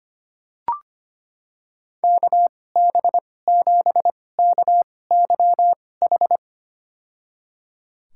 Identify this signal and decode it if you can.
Morse code, heard as a keyed tone, about 25 wpm, spaced as for 18 wpm, 700 Hz. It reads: KB7KYH
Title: USA Call Signs - 2x Character Spacing - 25wpm